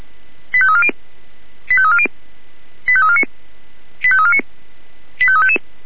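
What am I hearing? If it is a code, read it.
Five-tone selective call (ZVEI), heard as quick runs of five tones, tones of about 70 ms. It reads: 85340 85340 85349 85349 85355
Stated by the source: ZVEI five-tone selective call received by SDR.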